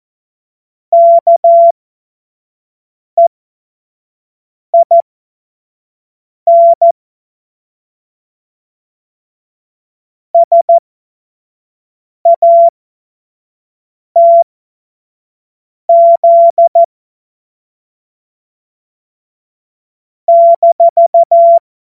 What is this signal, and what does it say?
Morse code, heard as a keyed tone, about 14 wpm, spaced as for 5 wpm, 680 Hz. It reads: KEIN SATZ -
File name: Morsekurs / Witze / Witz 15. Juni 2021, 25 F70